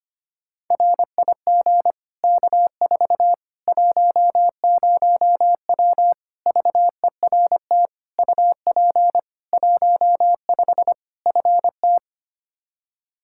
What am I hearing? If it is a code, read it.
Morse code, heard as a keyed tone, about 25 wpm, 700 Hz. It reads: RIG K4 10W VERT UP 15 FT
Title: Practice QSO - Exchange 2 - Info - 25wpm